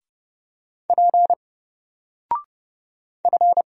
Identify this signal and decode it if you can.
Morse code, heard as a keyed tone, about 30 wpm, 700 Hz. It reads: P F
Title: Single Letter-Number Alt 30wpm